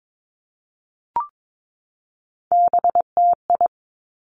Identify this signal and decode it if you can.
Morse code, heard as a keyed tone, about 22 wpm, 700 Hz. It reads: BTI
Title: Three Challenging Letter-Number Combinations 22wpm